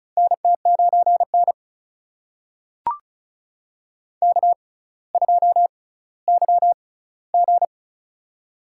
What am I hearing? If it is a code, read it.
Morse code, heard as a keyed tone, about 34 wpm, spaced as for 18 wpm, 700 Hz. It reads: NT9N K2YG
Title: USA Call Signs - 4 Characters - 6x Character Spacing - 35wpm